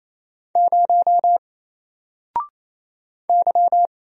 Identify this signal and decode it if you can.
Morse code, heard as a keyed tone, about 27 wpm, 700 Hz. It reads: 0 Y
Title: Single Letter-Number Alt 28wpm